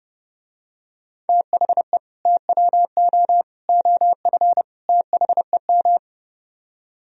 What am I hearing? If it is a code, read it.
Morse code, heard as a keyed tone, about 30 wpm, 700 Hz. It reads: THE TWO OF THEM